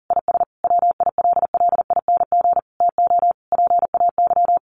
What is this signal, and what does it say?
Morse code, heard as a keyed tone, about 40 wpm, 700 Hz. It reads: IS WILLING TO PAY
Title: Sentences from Top 1400 Words 40wpm